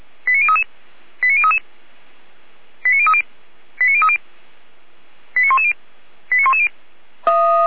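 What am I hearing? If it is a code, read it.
Five-tone selective call (ZVEI), heard as quick runs of five tones, tones of about 70 ms. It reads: 89033 89033 89030 89030 89110 89110
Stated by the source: ZVEI five-tone selective call 89110 received on SDR.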